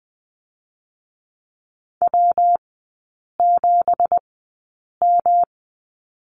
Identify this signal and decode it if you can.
Morse code, heard as a keyed tone, about 20 wpm, spaced as for 8 wpm, 690 Hz. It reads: W7M
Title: 166 Lesson 27e - Number 7 - Three Characters 20wpm